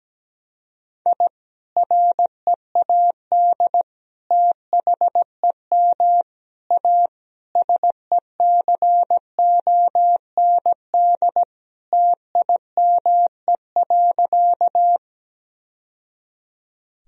Morse code, heard as a keyed tone, about 17 wpm, 700 Hz. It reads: I READ THEM A SECOND TIME.
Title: Sentences from Top 1000 Words - Review Part 1 - 17wpm